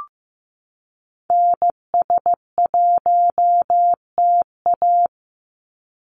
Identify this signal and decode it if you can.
Morse code, heard as a keyed tone, about 15 wpm, 700 Hz. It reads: NS1TA
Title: NA SOTA Activator Callsigns 15wpm